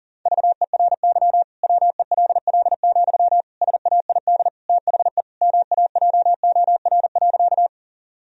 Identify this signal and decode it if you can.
Morse code, heard as a keyed tone, about 40 wpm, 700 Hz. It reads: VERY WELL, SAID THE MAJOR.
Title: Sentences from Top 1000 Words - Review Part 4 - 40wpm